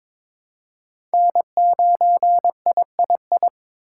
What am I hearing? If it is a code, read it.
Morse code, heard as a keyed tone, about 22 wpm, 700 Hz. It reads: N9III
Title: USA Call Signs - 3x Character Spacing - 22wpm